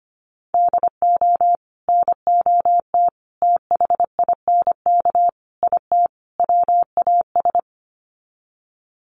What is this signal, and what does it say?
Morse code, heard as a keyed tone, about 25 wpm, 700 Hz. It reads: DO NOT THINK IT WAS
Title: Sets of 5 Words Encore 25wpm